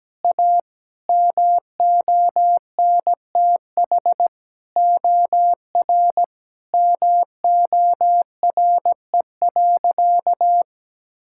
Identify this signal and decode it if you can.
Morse code, heard as a keyed tone, about 17 wpm, 700 Hz. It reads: A MONTH OR MORE.
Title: Sentences from Top 1000 Words - Review Part 5 - 17wpm